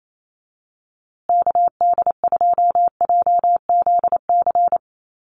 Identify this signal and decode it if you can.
Morse code, heard as a keyed tone, about 28 wpm, 700 Hz. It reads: KD2JZC
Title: USA Call Signs Repeated 2x 28wpm